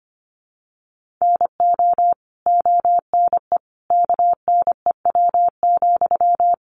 Morse code, heard as a keyed tone, about 25 wpm, 700 Hz. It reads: NO ONE KNEW,
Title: Sentences from Top 1400 Words 25wpm